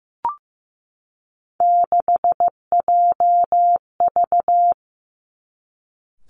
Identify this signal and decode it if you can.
Morse code, heard as a keyed tone, about 15 wpm, 700 Hz. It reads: 6JV
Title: Three Challenging Letter-Number Combinations 15wpm